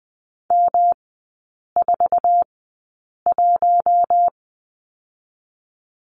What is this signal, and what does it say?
Morse code, heard as a keyed tone, about 20 wpm, spaced as for 8 wpm, 700 Hz. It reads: M41
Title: 107 Lesson 18e - Letter m - Three Characters 20wpm